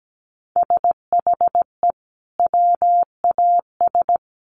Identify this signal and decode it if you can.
Morse code, heard as a keyed tone, about 17 wpm, 700 Hz. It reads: SHE WAS
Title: Sentences from Top 900 Words 17wpm